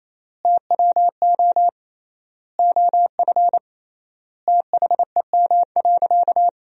Morse code, heard as a keyed tone, about 28 wpm, 700 Hz. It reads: TWO OF THEM.
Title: 3x Word-Spacing - Sentences from Top 100 Words - 28wpm